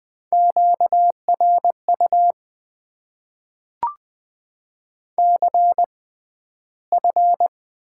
Morse code, heard as a keyed tone, about 20 wpm, spaced as for 12 wpm, 700 Hz. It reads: QRU CF
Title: Top 90 QSO Elements - 6x Character Spacing - 20wpm